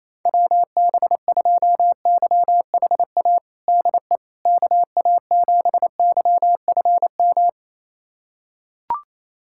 Morse code, heard as a keyed tone, about 28 wpm, 700 Hz. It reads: WB2YHA DE KA7YFM
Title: USA Callsign DE Callsign - 6 Characters - 28wpm